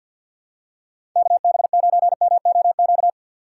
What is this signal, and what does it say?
Morse code, heard as a keyed tone, about 52 wpm, 700 Hz. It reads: KB9MOQ
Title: USA Call Signs - 3x Character Spacing - 50wpm